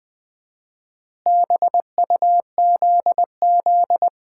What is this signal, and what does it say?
Morse code, heard as a keyed tone, about 20 wpm, 700 Hz. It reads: BUZZ
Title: Common QSO Phrases - 4x Word-Spacing - 20wpm